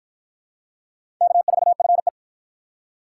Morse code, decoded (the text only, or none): K4FE